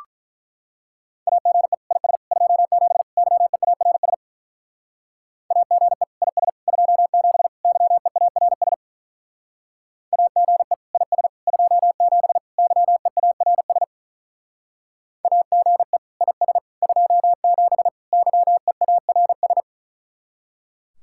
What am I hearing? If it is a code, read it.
Morse code, heard as a keyed tone, about 42 wpm, 700 Hz. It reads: AGE IS 27 YEARS AGE IS 27 YEARS AGE IS 27 YEARS AGE IS 27 YEARS
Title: Speed-Racing Common QSO Phrases Encore - 35wpm